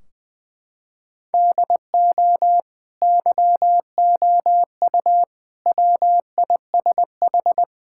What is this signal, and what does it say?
Morse code, heard as a keyed tone, about 20 wpm, 700 Hz. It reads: DO YOU WISH
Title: Sentences from Top 1600 Words 20wpm